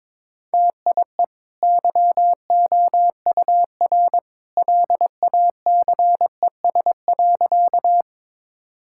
Morse code, heard as a keyed tone, about 22 wpm, 690 Hz. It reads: TIE YOUR LACES.
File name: practice / 22wpm / Easy Sentences with Hints - 22wpm